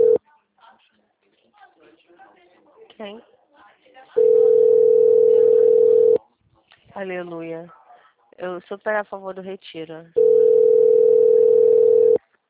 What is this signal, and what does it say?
Ringback tone (North America), 440 + 480 Hz, with speech.
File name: out- -2008- - - .3879